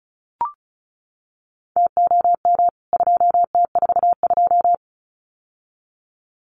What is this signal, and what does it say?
Morse code, heard as a keyed tone, about 34 wpm, 700 Hz. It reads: TOM 2T42
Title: CWT Exchange 35wpm